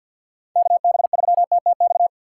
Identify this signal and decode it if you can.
Morse code, heard as a keyed tone, about 48 wpm, 700 Hz. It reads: KB3TTX
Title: USA Call Signs - 6 Characters - 6x Character Spacing - 50wpm